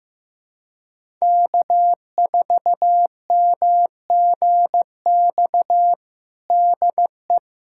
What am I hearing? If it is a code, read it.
Morse code, heard as a keyed tone, about 15 wpm, 700 Hz. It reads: K4MGX DE
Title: USA Callsign DE Callsign - 5 Characters - 15wpm